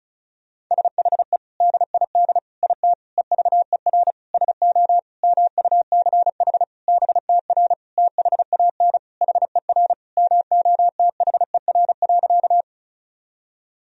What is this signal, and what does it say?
Morse code, heard as a keyed tone, about 34 wpm, 700 Hz. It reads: SHE DID IT EVER SO MUCH BTR THAN HER MOTHER.
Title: Sentences from Top 400 Words and Abbreviations 35wpm